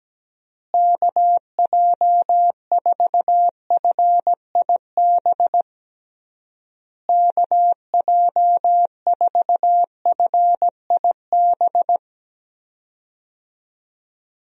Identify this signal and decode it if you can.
Morse code, heard as a keyed tone, about 17 wpm, 700 Hz. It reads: KJ4FIB KJ4FIB
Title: USA Call Signs - 6 Characters - Repeated 3x - 17wpm